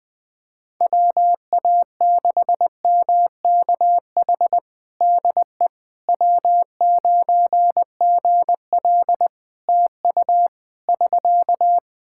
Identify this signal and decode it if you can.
Morse code, heard as a keyed tone, about 21 wpm, 700 Hz. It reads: WA6MKH DE W9GL TU <SK>
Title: Practice QSO - Exchange 4 - Ending - 20wpm